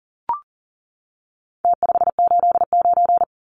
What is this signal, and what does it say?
Morse code, heard as a keyed tone, about 40 wpm, 700 Hz. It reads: T589
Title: Numbers - Contest Style - 4-Digit 40wpm